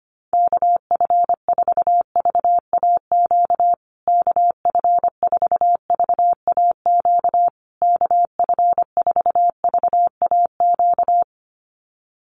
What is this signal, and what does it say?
Morse code, heard as a keyed tone, about 24 wpm, 700 Hz. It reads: KF4VAQ KF4VAQ KF4VAQ
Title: USA Call Signs Repeated 3x 25wpm